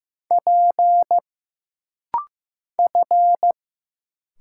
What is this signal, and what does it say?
Morse code, heard as a keyed tone, about 15 wpm, 700 Hz. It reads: P F
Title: Single Letter-Number Alt - Rapid-Fire 15wpm